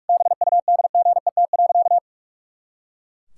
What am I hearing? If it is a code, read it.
Morse code, heard as a keyed tone, about 41 wpm, 700 Hz. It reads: BUDGET.
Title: Pangram Challenge v2 - 45wpm